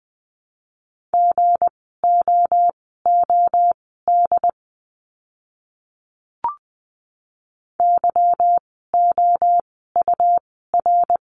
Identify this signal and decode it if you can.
Morse code, heard as a keyed tone, about 20 wpm, spaced as for 14 wpm, 700 Hz. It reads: GOOD YOUR
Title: 206 Lesson 33f - Letter j - Words 20wpm